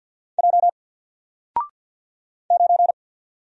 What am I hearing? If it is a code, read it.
Morse code, heard as a keyed tone, about 50 wpm, 700 Hz. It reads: J 9
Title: Single Letter-Number 50wpm